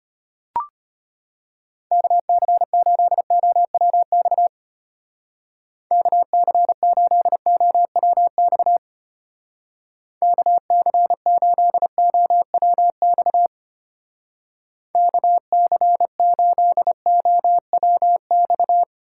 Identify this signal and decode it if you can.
Morse code, heard as a keyed tone, about 31 wpm, 700 Hz. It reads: KC8OWX KC8OWX KC8OWX KC8OWX